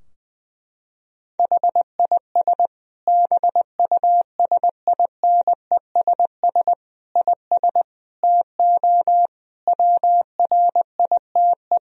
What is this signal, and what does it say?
Morse code, heard as a keyed tone, about 20 wpm, 700 Hz. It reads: HIS BUSINESS IS TO WRITE
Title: Sentences from Top 1000 Words - Review Part 3 - 20wpm